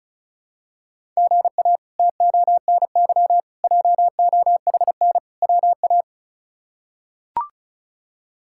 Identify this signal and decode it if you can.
Morse code, heard as a keyed tone, about 34 wpm, 700 Hz. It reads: GA TONY JOHN WA
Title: K1USN SST QSO Practice - 35wpm